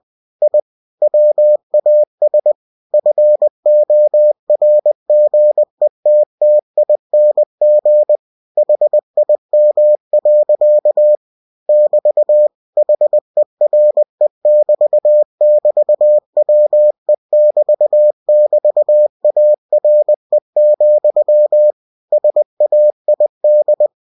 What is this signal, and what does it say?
Morse code, heard as a keyed tone, about 20 wpm, 600 Hz. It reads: I WAS FORGETTING HIM. <BT> HERE<BT><BT>WE<BT><BT>ARE, SAID